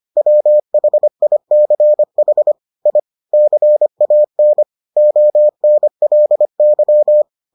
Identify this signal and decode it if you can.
Morse code, heard as a keyed tone, about 25 wpm, 600 Hz. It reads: WHICH I CAN ONLY